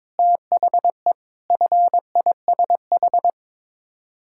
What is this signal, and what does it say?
Morse code, heard as a keyed tone, about 22 wpm, 700 Hz. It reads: THE FISH
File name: practice / 22wpm / Sets of 8 Words - 22wpm